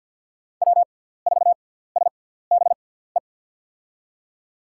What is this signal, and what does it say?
Morse code, heard as a keyed tone, about 51 wpm, spaced as for 17 wpm, 700 Hz. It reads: W4SBE
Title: USA Call Signs - 5 Characters - 6x Character Spacing - 50wpm